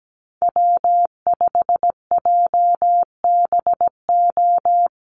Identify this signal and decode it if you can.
Morse code, heard as a keyed tone, about 17 wpm, 700 Hz. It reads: W5JBO
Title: Call Signs Repeated 3x - 17wpm